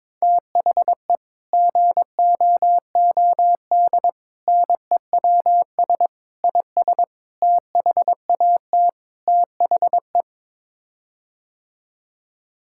Morse code, heard as a keyed tone, about 21 wpm, 700 Hz. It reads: THE GOOD NEWS IS THAT THE
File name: practice / 22wpm / Sets of 6 Words - 22wpm